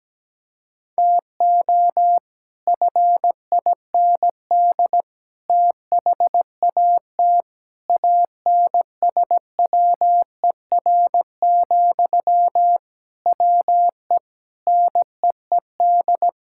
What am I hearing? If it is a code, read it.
Morse code, heard as a keyed tone, about 17 wpm, 700 Hz. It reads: TO FIND THAT ANSWER, WE NEED
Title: Sentences from Top 1600 Words 17wpm